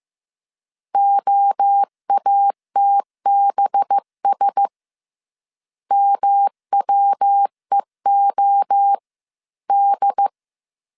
Morse code, heard as a keyed tone, about 15 wpm, 800 Hz. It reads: OATBS MWEO D
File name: CW / DET / 1 6 1500C S C 14wpm 2951